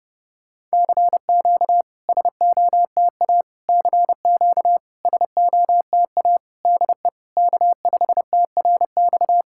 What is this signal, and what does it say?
Morse code, heard as a keyed tone, about 30 wpm, 700 Hz. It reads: CQ SOTA CQ SOTA DE K5TRX
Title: SOTA QSO Practice 30wpm